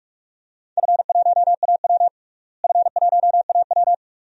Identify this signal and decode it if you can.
Morse code, heard as a keyed tone, about 43 wpm, 700 Hz. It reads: F1AW F1AW
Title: DXCC-100 Call Signs - 4 Characters - Repeated 2x - 45wpm